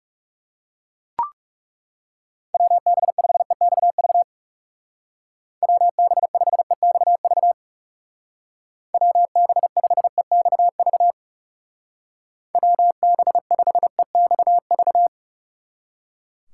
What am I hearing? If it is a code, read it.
Morse code, heard as a keyed tone, about 36 wpm, 700 Hz. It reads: WB5EXV WB5EXV WB5EXV WB5EXV
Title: Speed Racing - USA 2x3 Call Signs 30wpm